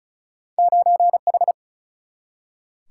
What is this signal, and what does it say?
Morse code, heard as a keyed tone, about 36 wpm, 700 Hz. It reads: 9H
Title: Single-Letter Single-Number 35wpm